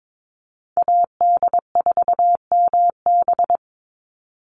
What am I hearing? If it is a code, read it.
Morse code, heard as a keyed tone, about 22 wpm, 700 Hz. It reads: AD4MB
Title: USA Call Signs 22wpm